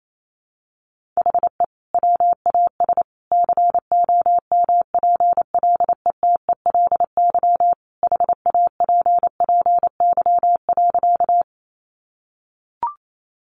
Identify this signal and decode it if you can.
Morse code, heard as a keyed tone, about 28 wpm, 700 Hz. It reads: HE WAS COMPLETELY HAPPY.
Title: Sentences from Top 1100 Words 28wpm